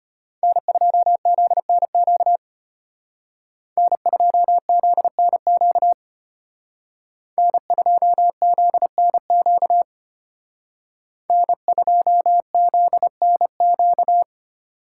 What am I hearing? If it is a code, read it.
Morse code, heard as a keyed tone, about 32 wpm, 700 Hz. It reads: N2ZNQ N2ZNQ N2ZNQ N2ZNQ